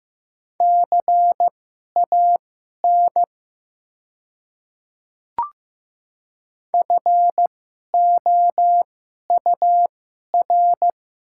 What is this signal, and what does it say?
Morse code, heard as a keyed tone, about 15 wpm, spaced as for 11 wpm, 700 Hz. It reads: CAN FOUR
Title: 2x Character Spacing - Top 300 Words - 15wpm